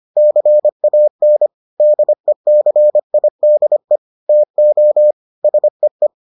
Morse code, heard as a keyed tone, about 25 wpm, 600 Hz. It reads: CAN DECIDE TO SEE